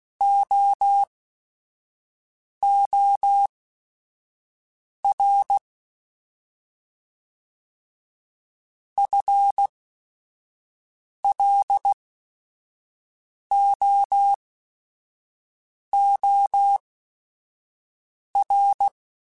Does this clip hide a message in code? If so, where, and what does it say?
Yes: Morse code from 0.2 to 18.9 s, heard as a keyed tone, about 16 wpm, spaced as for 5 wpm, 800 Hz. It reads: OOR FLOOR